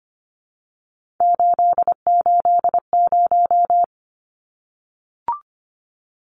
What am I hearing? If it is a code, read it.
Morse code, heard as a keyed tone, about 25 wpm, 700 Hz. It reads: 880
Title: Numbers - 3-Digit 25wpm